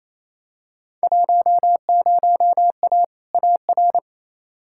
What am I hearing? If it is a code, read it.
Morse code, heard as a keyed tone, about 28 wpm, 700 Hz. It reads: 10A AR
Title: Field Day Exchanges Encore - 28wpm